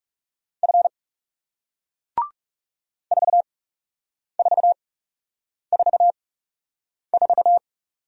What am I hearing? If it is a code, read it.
Morse code, heard as a keyed tone, about 38 wpm, spaced as for 8 wpm, 700 Hz. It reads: F 4444